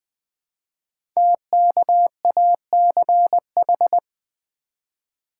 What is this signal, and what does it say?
Morse code, heard as a keyed tone, about 20 wpm, 700 Hz. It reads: TKACH
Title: Code Groups - 5 Letters - 20wpm